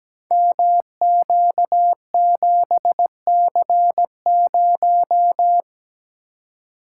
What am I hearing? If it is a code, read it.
Morse code, heard as a keyed tone, about 17 wpm, 690 Hz. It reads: MQ7C0
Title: Code Groups - 5 Letter:Numbers - 17wpm